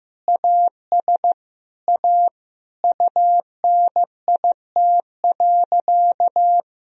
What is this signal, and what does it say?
Morse code, heard as a keyed tone, about 15 wpm, 700 Hz. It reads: AS A UNIT.
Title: Sentences from Top 1000 Words - Review Part 2 - 15wpm